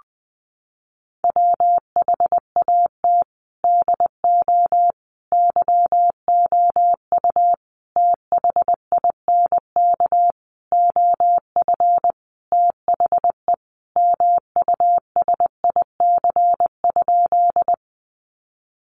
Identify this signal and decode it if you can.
Morse code, heard as a keyed tone, about 20 wpm, 700 Hz. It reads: WHAT DO YOU THINK OF THE MUSIC?
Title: Sentences from Top 300 Words 20wpm